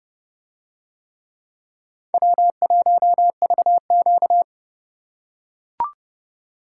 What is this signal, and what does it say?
Morse code, heard as a keyed tone, about 28 wpm, 690 Hz. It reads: W1VQ